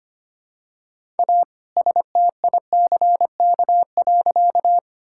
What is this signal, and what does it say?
Morse code, heard as a keyed tone, about 25 wpm, 700 Hz. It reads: A STICK.
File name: practice / 25wpm / All-Age Appropriate Jokes - 25wpm